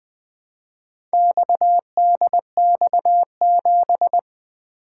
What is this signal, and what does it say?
Morse code, heard as a keyed tone, about 20 wpm, 700 Hz. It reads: XDX7
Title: Four Letter-Number Combinations - 20wpm